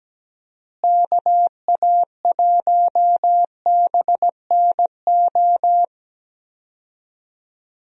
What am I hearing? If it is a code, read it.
Morse code, heard as a keyed tone, about 17 wpm, 700 Hz. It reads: KA1BNO
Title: USA Call Signs - 6 Characters - Repeated 3x - 17wpm